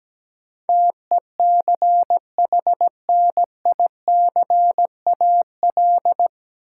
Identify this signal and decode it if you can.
Morse code, heard as a keyed tone, about 17 wpm, 700 Hz. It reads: TECHNICAL
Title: Sentences from Top 1100 Words - New Word Drill - 17wpm